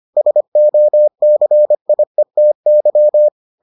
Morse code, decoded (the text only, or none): SOCIETY